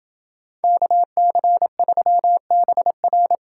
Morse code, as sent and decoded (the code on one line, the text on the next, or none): -.- -.-. ...-- -... .-.
KC3BR